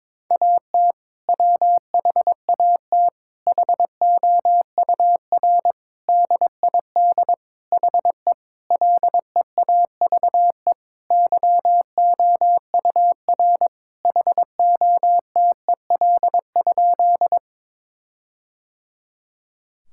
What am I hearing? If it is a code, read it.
Morse code, heard as a keyed tone, about 22 wpm, 700 Hz. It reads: AT WHAT HOUR DID HE LEAVE YOUR HOTEL?
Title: Sentences from Top 1600 Words 22wpm